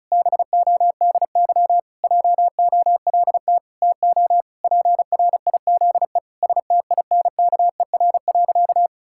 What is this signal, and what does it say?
Morse code, heard as a keyed tone, about 36 wpm, 700 Hz. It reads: BODY JOLT TO PRIZE STINKER.
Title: Pangram Challenge v2 - 35wpm